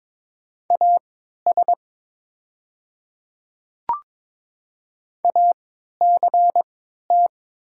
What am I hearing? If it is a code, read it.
Morse code, heard as a keyed tone, about 22 wpm, spaced as for 13 wpm, 700 Hz. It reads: AS ACT
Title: 3x Character Spacing - Top 500 Words - 22wpm